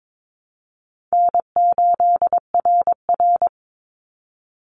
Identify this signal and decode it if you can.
Morse code, heard as a keyed tone, about 22 wpm, 700 Hz. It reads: N8RR
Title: Easy Call Signs Repeated 3x - 22wpm